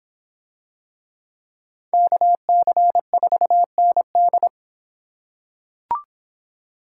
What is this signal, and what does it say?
Morse code, heard as a keyed tone, about 23 wpm, 700 Hz. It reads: KC4ND